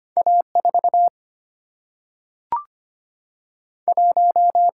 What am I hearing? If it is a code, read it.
Morse code, heard as a keyed tone, about 25 wpm, 700 Hz. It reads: A 4 1
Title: Single-Letter Single-Number 25wpm